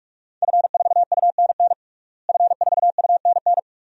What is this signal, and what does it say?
Morse code, heard as a keyed tone, about 46 wpm, 700 Hz. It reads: F4UNN F4UNN
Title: DXCC-100 Call Signs - 5 Characters - Repeated 2x - 45wpm